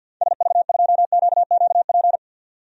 Morse code, heard as a keyed tone, about 48 wpm, 700 Hz. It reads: SV2QQP
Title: DXCC-100 Call Signs - 6 Characters - 50wpm